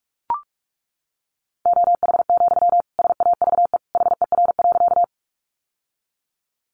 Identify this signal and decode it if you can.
Morse code, heard as a keyed tone, about 44 wpm, 700 Hz. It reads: OH, SAVE HER.
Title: Sentences from Top 1100 Words 45wpm